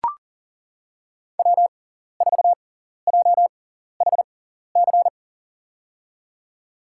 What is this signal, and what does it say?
Morse code, heard as a keyed tone, about 40 wpm, spaced as for 14 wpm, 700 Hz. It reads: W4JHC
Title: USA Call Signs - 6x Character Spacing - 40wpm